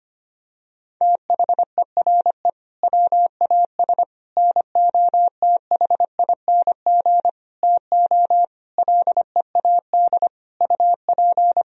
Morse code, heard as a keyed tone, about 25 wpm, 700 Hz. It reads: THERE WAS NOTHING TO LEAD UP
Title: Sentences from Top 1000 Words - Review Part 2 - 25wpm